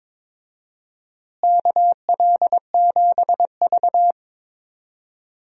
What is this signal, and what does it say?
Morse code, heard as a keyed tone, about 22 wpm, 700 Hz. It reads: KL7V
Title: DXCC-100 Call Signs - 4 Characters - Repeated 2x - 22wpm